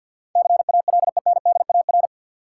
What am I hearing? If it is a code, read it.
Morse code, heard as a keyed tone, about 52 wpm, 700 Hz. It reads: CALENDAR
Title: Sentences from Top 1600 Words - New Word Drill - 50wpm